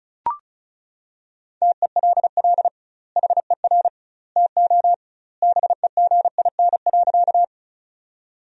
Morse code, heard as a keyed tone, about 36 wpm, 700 Hz. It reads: TELL HER TO BEGIN.
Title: 2x Word-Spacing - Sentences from Top 300 Words - 35wpm